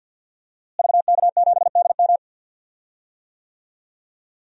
Morse code, heard as a keyed tone, about 46 wpm, 700 Hz. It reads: VK7DM
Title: DXCC-100 Call Signs - 5 Characters - Repeated 2x - 50wpm